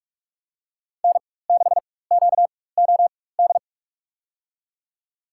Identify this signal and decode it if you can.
Morse code, heard as a keyed tone, about 46 wpm, spaced as for 21 wpm, 700 Hz. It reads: N6QOD
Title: USA Call Signs - 4x Character Spacing - 45wpm